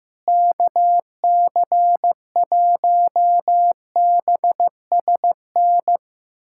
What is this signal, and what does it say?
Morse code, heard as a keyed tone, about 15 wpm, 700 Hz. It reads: KC1BSN
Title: USA Callsign DE Callsign 15wpm